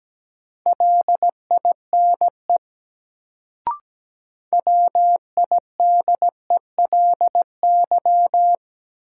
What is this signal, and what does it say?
Morse code, heard as a keyed tone, about 17 wpm, 700 Hz. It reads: LINE WIDELY
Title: Sentences from Top 1200 Words - New Word Drill Alt - 17wpm